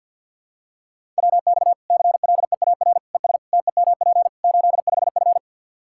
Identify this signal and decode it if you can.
Morse code, heard as a keyed tone, about 52 wpm, 700 Hz. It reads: WX CLEAR ES TEMP 85F
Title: Practice QSO - Exchange 2 - Info - 50wpm